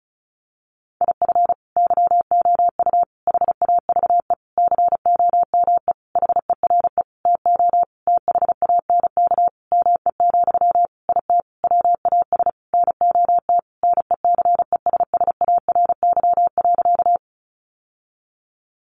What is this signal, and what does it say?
Morse code, heard as a keyed tone, about 34 wpm, 700 Hz. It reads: IF YOU HAVE COME HERE TO THANK ME, IT WAS NOT NECESSARY.
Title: Sentences from Top 800 Words 35wpm